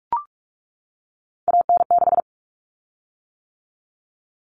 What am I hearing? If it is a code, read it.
Morse code, heard as a keyed tone, about 44 wpm, 700 Hz. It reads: AN6
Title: Common Cut Numbers - 3-Digit 45wpm